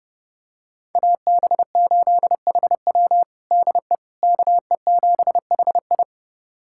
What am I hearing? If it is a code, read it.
Morse code, heard as a keyed tone, about 32 wpm, 700 Hz. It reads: AB8HW DE KE7HI
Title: USA Callsign DE Callsign - 5 Characters - 30wpm